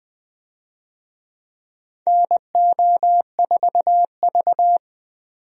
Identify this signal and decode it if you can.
Morse code, heard as a keyed tone, about 20 wpm, 700 Hz. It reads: NO4V
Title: USA Call Signs - 4 Characters - 20wpm